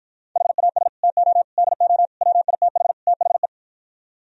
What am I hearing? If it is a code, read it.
Morse code, heard as a keyed tone, about 52 wpm, 700 Hz. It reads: HAS TO DO WITH THE